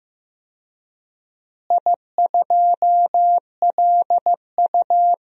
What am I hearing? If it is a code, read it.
Morse code, heard as a keyed tone, about 15 wpm, 700 Hz. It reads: I2LU
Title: DXCC-100 Call Signs - 4 Characters - Repeated 2x - 15wpm